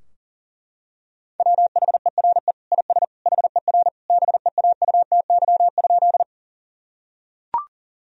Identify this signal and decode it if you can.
Morse code, heard as a keyed tone, about 40 wpm, 700 Hz. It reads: WHERE IS HER BEAUTY?
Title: Sentences from Top 1000 Words - Review Part 2 - 40wpm